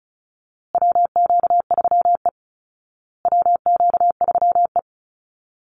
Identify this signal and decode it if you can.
Morse code, heard as a keyed tone, about 34 wpm, 700 Hz. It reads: WQ3E WQ3E
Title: Call Signs Repeated 2x - 35wpm